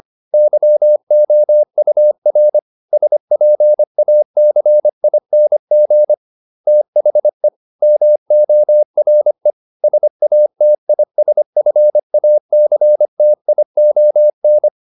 Morse code, heard as a keyed tone, about 25 wpm, 600 Hz. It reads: YOUR SPACING THE MORE SATISFACTION